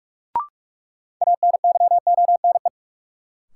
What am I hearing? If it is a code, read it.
Morse code, heard as a keyed tone, about 46 wpm, 700 Hz. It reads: ANYONE
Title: Sentences from Top 900 Words - New Word Drill - 45wpm